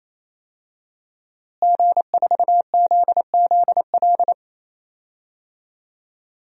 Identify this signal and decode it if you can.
Morse code, heard as a keyed tone, about 28 wpm, 700 Hz. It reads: G4ZZL
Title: DXCC-100 Call Signs - 5 Characters - Repeated 3x - 28wpm